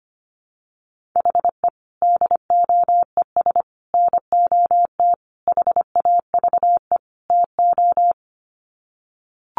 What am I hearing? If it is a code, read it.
Morse code, heard as a keyed tone, about 25 wpm, 700 Hz. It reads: HE DOES NOT HAVE TO
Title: Sets of 5 Words Encore 25wpm